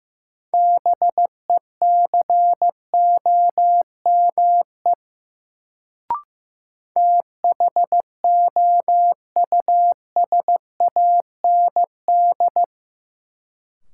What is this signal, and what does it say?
Morse code, heard as a keyed tone, about 15 wpm, 700 Hz. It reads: BECOME THOUSAND
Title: Sentences from Top 600 Words - New Word Drill - 15wpm